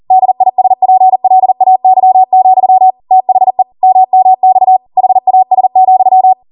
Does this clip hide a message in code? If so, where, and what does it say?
Yes: Morse code from 0.1 to 6.3 s, heard as a keyed tone, about 40 wpm, 750 Hz. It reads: DISPLAY, THE MMX HAS,